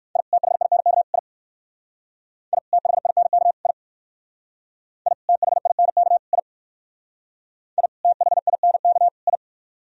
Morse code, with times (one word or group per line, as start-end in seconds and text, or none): 0.2-0.2 I
0.3-1.0 THINK
1.1-1.2 I
2.5-2.6 I
2.7-3.5 THINK
3.7-3.7 I
5.1-5.1 I
5.3-6.2 THINK
6.3-6.4 I
7.8-7.9 I
8.0-9.1 THINK
9.3-9.4 I